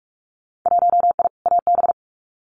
Morse code, heard as a keyed tone, about 46 wpm, 700 Hz. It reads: 1I AB
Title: Winter Field Day Exchanges Encore 45wpm